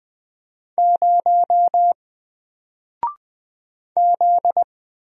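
Morse code, heard as a keyed tone, about 20 wpm, 700 Hz. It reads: 0 Z
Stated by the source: Single letter and number Morse drill at 20 wpm, from N4LCD.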